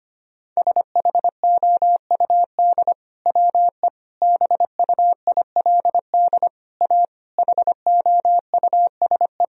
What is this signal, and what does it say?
Morse code, heard as a keyed tone, about 25 wpm, 700 Hz. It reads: SHOUD WE BUILD A HOUSE
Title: Sentences from Top 1000 Words - Review Part 4 - 25wpm